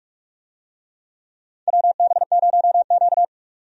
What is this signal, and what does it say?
Morse code, heard as a keyed tone, about 46 wpm, 700 Hz. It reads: WB0Q
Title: USA Call Signs - 4 Characters - 45wpm